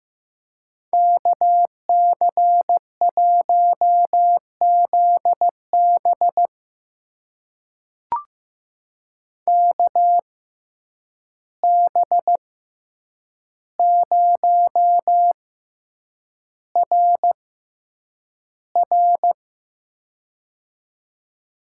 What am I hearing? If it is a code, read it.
Morse code, heard as a keyed tone, about 15 wpm, spaced as for 8 wpm, 700 Hz. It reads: KC1ZB KB0RR
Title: USA Call Signs - 5 Characters - 6x Character Spacing - 15wpm